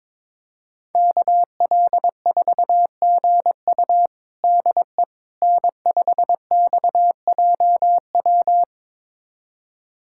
Morse code, heard as a keyed tone, about 22 wpm, 700 Hz. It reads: KL4GU DE N5XJW
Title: USA Callsign DE Callsign - 5 Characters - 22wpm